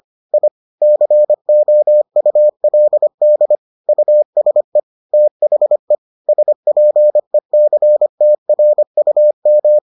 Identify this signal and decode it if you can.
Morse code, heard as a keyed tone, about 25 wpm, 600 Hz. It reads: I COULD USE THE SPECTRUM